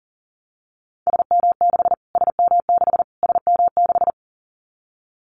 Morse code, heard as a keyed tone, about 40 wpm, 700 Hz. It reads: SM6 SM6 SM6
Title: Callsign Prefixes for Top 25 DXCC - Repeated 3x - 40wpm